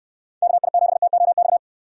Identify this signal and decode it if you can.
Morse code, heard as a keyed tone, about 65 wpm, 700 Hz. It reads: KI7TOX